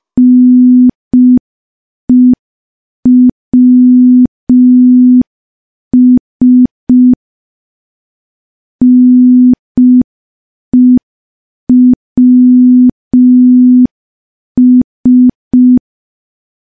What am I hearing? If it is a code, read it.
Morse code, heard as a keyed tone, about 5 wpm, 250 Hz. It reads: NEWS NEWS
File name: morse / morse-training-05WPM